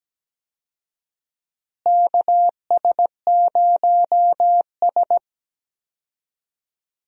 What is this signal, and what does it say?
Morse code, heard as a keyed tone, about 17 wpm, 700 Hz. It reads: KS0S
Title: USA Call Signs - 4 Characters - Repeated 2x - 17wpm